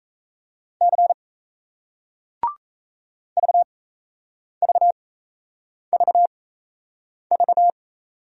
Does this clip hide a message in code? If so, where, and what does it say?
Yes: Morse code from 0.8 to 7.7 s, heard as a keyed tone, about 38 wpm, spaced as for 8 wpm, 700 Hz. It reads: C VVVV